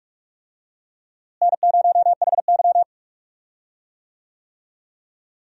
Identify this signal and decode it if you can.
Morse code, heard as a keyed tone, about 46 wpm, 700 Hz. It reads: N0HY
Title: USA Call Signs - 4 Characters - Repeated 3x - 45wpm